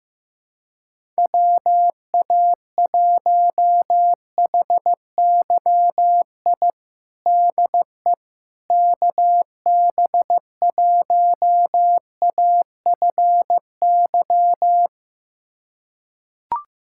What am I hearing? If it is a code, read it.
Morse code, heard as a keyed tone, about 15 wpm, 700 Hz. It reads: WA1HYI DE KB1AFY
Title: USA Callsign DE Callsign - 6 Characters - 15wpm